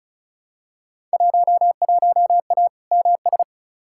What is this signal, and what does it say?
Morse code, heard as a keyed tone, about 34 wpm, 700 Hz. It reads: 11A MS
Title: Field Day Exchanges Encore - 35wpm